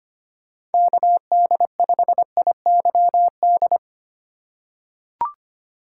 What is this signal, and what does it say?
Morse code, heard as a keyed tone, about 25 wpm, 700 Hz. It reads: KD5IYD
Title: USA Call Signs - 6 Characters - 2x Character Spacing - 25wpm